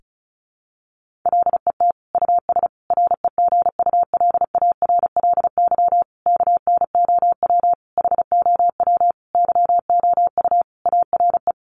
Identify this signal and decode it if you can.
Morse code, heard as a keyed tone, about 34 wpm, 700 Hz. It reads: LET US REGULARLY KNOW HOW YOU ARE
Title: Sentences from Top 1400 Words 35wpm